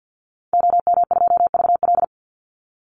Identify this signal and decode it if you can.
Morse code, heard as a keyed tone, about 50 wpm, 700 Hz. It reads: OM2VL
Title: Call Signs Repeated 3x - 50wpm